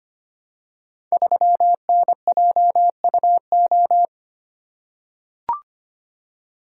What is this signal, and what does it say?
Morse code, heard as a keyed tone, about 25 wpm, 700 Hz. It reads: 3NJUO
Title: Code Groups - 5 Letter:Numbers - 25wpm